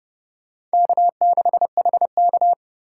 Morse code, heard as a keyed tone, about 29 wpm, 700 Hz. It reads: K6HK